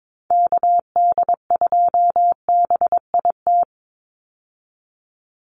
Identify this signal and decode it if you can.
Morse code, heard as a keyed tone, about 22 wpm, 700 Hz. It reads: KD2BIT
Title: USA Call Signs 22wpm